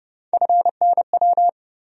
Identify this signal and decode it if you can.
Morse code, heard as a keyed tone, about 30 wpm, 700 Hz. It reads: FNW
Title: USA Call Sign Postfixes - 3 Letters 30wpm